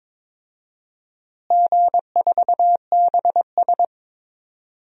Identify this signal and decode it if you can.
Morse code, heard as a keyed tone, about 22 wpm, 700 Hz. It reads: G4BS
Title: DXCC-100 Call Signs - 4 Characters - Repeated 2x - 22wpm